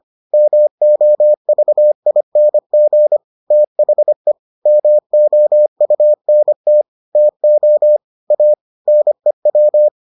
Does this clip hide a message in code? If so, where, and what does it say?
Yes: Morse code from 0.3 to 9.9 s, heard as a keyed tone, about 25 wpm, 600 Hz. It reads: MOVING THE MOUNT TO A NEW